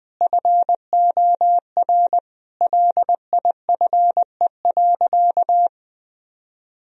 Morse code, heard as a keyed tone, about 20 wpm, 700 Hz. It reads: FOR LIFE.
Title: Sentences from Top 1000 Words - Review Part 4 - 20wpm